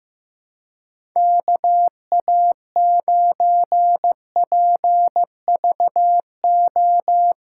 Morse code, heard as a keyed tone, about 15 wpm, 700 Hz. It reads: KA9PVO
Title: USA Callsign DE Callsign - 6 Characters - 15wpm